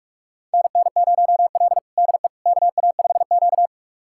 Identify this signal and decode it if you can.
Morse code, heard as a keyed tone, about 41 wpm, 700 Hz. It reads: NN0L DE KA5Q
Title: USA Callsign DE Callsign - 4 Characters - 45wpm